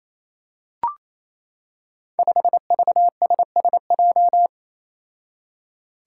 Morse code, heard as a keyed tone, about 28 wpm, 700 Hz. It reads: 5VSSJ
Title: Five Challenging Letter-Number Combinations 28wpm